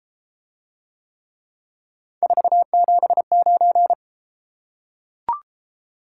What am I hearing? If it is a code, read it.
Morse code, heard as a keyed tone, about 33 wpm, 700 Hz. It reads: 479